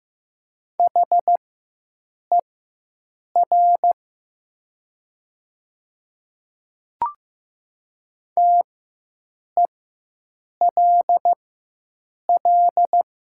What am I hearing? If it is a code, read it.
Morse code, heard as a keyed tone, about 15 wpm, spaced as for 7 wpm, 700 Hz. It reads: HER TELL
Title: 4x Character Spacing - Top 500 Words - 15wpm